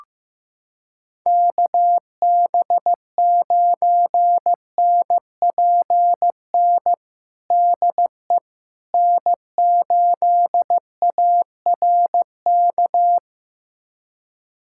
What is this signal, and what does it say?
Morse code, heard as a keyed tone, about 15 wpm, 700 Hz. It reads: KB9NPN DE N8ARK
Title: USA Callsign DE Callsign 15wpm